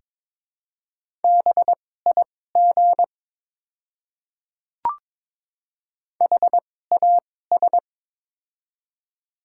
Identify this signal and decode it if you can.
Morse code, heard as a keyed tone, about 22 wpm, spaced as for 16 wpm, 700 Hz. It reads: BIG HAS
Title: 2x Character Spacing - Top 300 Words - 22wpm